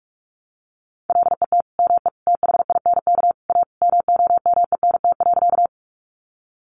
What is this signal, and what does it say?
Morse code, heard as a keyed tone, about 44 wpm, 700 Hz. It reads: LET ME THINK A MOMENT.
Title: Sentences from Top 1000 Words 45wpm